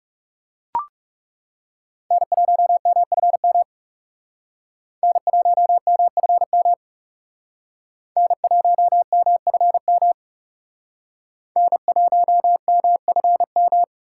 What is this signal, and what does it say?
Morse code, heard as a keyed tone, about 39 wpm, 700 Hz. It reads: N1MFM N1MFM N1MFM N1MFM